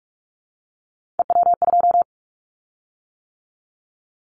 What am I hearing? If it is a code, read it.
Morse code, heard as a keyed tone, about 46 wpm, 700 Hz. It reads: EW2
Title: Callsign Prefixes for Top 50 DXCC - 45wpm